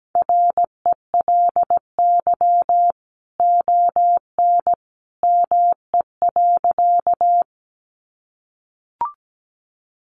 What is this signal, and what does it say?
Morse code, heard as a keyed tone, about 17 wpm, 700 Hz. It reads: RELY ON ME.
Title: Sentences from Top 1200 Words 17wpm